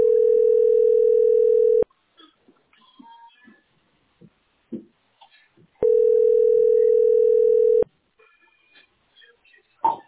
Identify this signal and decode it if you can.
Ringback tone (North America), 440 + 480 Hz.